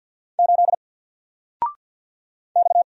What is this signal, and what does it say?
Morse code, heard as a keyed tone, about 48 wpm, 700 Hz. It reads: 8 X
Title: Single Letter-Number Alt 50wpm